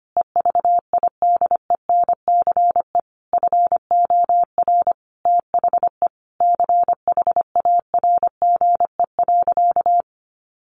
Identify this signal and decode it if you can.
Morse code, heard as a keyed tone, about 25 wpm, 700 Hz. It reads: EVIDENCE FOR THE CHARGE.
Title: Sentences from Top 1300 Words 25wpm